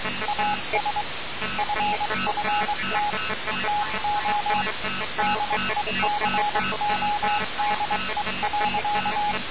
Morse code, heard as a keyed tone, about 23 wpm, 830 Hz, with noise. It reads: AS FFT TO DECODE.
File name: radiocap / REC z